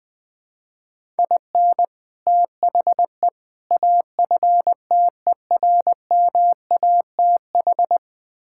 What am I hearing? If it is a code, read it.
Morse code, heard as a keyed tone, about 21 wpm, 700 Hz. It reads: IN THE AFTERMATH
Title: Build A Phrase - 5-Word Phrases - 20wpm